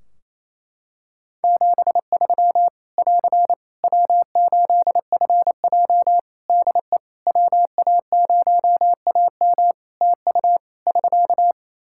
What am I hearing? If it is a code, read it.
Morse code, heard as a keyed tone, about 28 wpm, 700 Hz. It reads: 73 <AR> W8FJ DE WA0AM TU <SK>
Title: Practice QSO - Exchange 4 - Ending - 28wpm